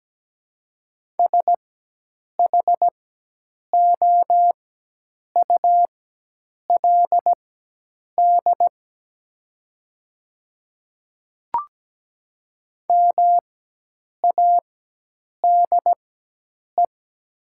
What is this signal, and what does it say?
Morse code, heard as a keyed tone, about 17 wpm, spaced as for 8 wpm, 700 Hz. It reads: SHOULD MADE
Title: 4x Character Spacing - Top 300 Words - 17wpm